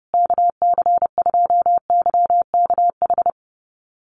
Morse code, heard as a keyed tone, about 30 wpm, 700 Hz. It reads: KC2YKH
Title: USA Call Signs Repeated 2x 30wpm